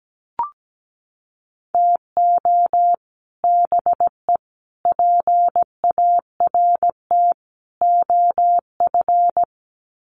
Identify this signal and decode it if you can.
Morse code, heard as a keyed tone, about 17 wpm, 700 Hz. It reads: TO BE PART OF
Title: Sets of 4 Words 17wpm